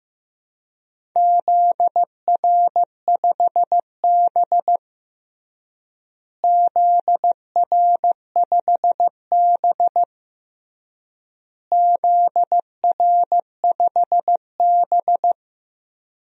Morse code, heard as a keyed tone, about 15 wpm, 700 Hz. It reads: ZR5B ZR5B ZR5B
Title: DXCC-100 Call Signs - 4 Characters - Repeated 3x - 15wpm